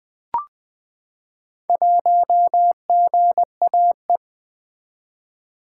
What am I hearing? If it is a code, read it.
Morse code, heard as a keyed tone, about 20 wpm, 700 Hz. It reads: 1GAE
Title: Four Letter-Number Combinations - 20wpm